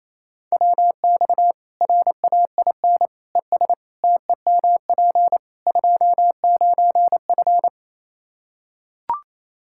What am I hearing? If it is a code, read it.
Morse code, heard as a keyed tone, about 27 wpm, 700 Hz. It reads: WX RAIN ES TEMP 29F
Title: Practice QSO - Exchange 2 - Info - 28wpm